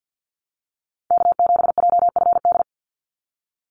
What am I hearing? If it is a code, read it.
Morse code, heard as a keyed tone, about 48 wpm, 700 Hz. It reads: K7JFD
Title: NA SOTA Activator Callsigns 50wpm